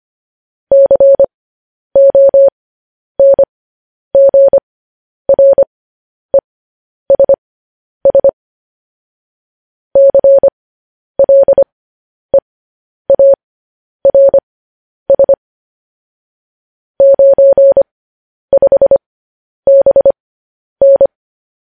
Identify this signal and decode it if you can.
Morse code, heard as a keyed tone, about 25 wpm, spaced as for 10 wpm, 550 Hz. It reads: CONGRESS CLEARS 95BN